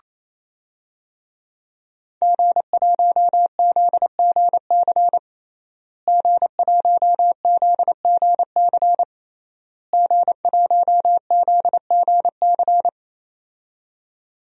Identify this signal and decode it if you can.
Morse code, heard as a keyed tone, about 28 wpm, 700 Hz. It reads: G1ZGC G1ZGC G1ZGC
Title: DXCC-100 Call Signs - 5 Characters - Repeated 3x - 28wpm